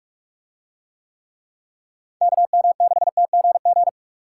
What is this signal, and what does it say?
Morse code, heard as a keyed tone, about 44 wpm, 700 Hz. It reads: KM6TGG